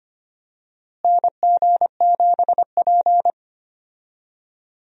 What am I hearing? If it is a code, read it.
Morse code, heard as a keyed tone, about 25 wpm, 700 Hz. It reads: NG7P
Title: USA Call Signs - 4 Characters - Repeated 2x - 25wpm